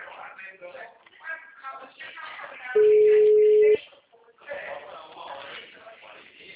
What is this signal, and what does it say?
Ringback tone (Europe, 425 Hz), with speech.
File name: out- -2007- - - .50786